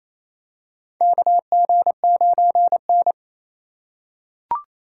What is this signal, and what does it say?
Morse code, heard as a keyed tone, about 28 wpm, 700 Hz. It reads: KG9N
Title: USA Call Signs - 4 Characters - 3x Character Spacing - 28wpm